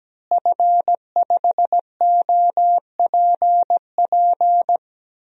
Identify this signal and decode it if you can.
Morse code, heard as a keyed tone, about 17 wpm, 700 Hz. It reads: F5OPP
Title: DXCC-100 Call Signs - 5 Characters - Repeated 2x - 17wpm